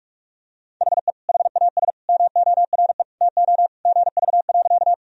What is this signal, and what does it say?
Morse code, heard as a keyed tone, about 42 wpm, 700 Hz. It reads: HE HAS MORE TO GV.
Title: Sentences from Top 200 Words and Abbreviations 45wpm